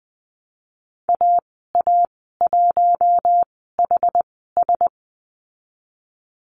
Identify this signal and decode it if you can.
Morse code, heard as a keyed tone, about 20 wpm, spaced as for 14 wpm, 700 Hz. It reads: AA1HS
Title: 088 Lesson 15g - Number 5 - Call Signs 20wpm